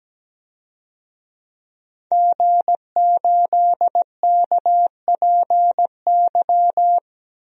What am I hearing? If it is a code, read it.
Morse code, heard as a keyed tone, about 17 wpm, 700 Hz. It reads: G8KPY
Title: DXCC-100 Call Signs - 5 Characters - Repeated 2x - 17wpm